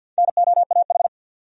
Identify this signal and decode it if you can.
Morse code, heard as a keyed tone, about 49 wpm, 700 Hz. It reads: NOAH
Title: USA Common Names - Max 4 Letters - 50wpm